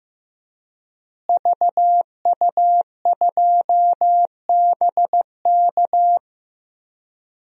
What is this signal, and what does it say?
Morse code, heard as a keyed tone, about 15 wpm, 700 Hz. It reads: VU2BK
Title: DXCC-100 Call Signs - 5 Characters - Repeated 3x - 15wpm